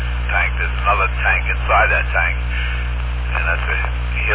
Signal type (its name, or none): Emergency Alert System (SAME) tones